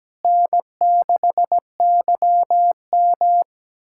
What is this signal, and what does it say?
Morse code, heard as a keyed tone, about 17 wpm, 700 Hz. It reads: N6YM
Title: USA Callsign DE Callsign - 4 Characters - 17wpm